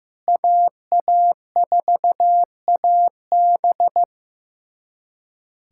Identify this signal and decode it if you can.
Morse code, heard as a keyed tone, about 16 wpm, 700 Hz. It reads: AA4AB
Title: USA Callsign DE Callsign - 5 Characters - 15wpm